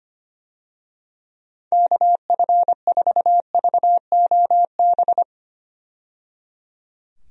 Morse code, heard as a keyed tone, about 25 wpm, 700 Hz. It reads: KF4VOB
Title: USA Call Signs - 6 Characters - 25wpm